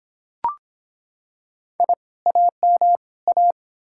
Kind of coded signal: Morse code

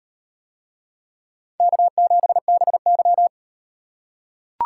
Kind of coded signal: Morse code

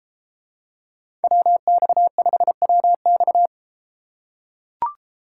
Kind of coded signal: Morse code

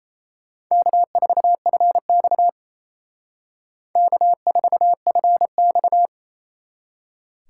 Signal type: Morse code

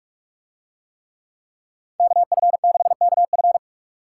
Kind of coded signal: Morse code

